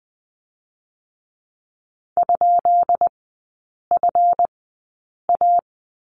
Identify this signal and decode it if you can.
Morse code, heard as a keyed tone, about 20 wpm, spaced as for 8 wpm, 700 Hz. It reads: ?FA
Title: 138 Lesson 23e - Letter f - Three Characters 20wpm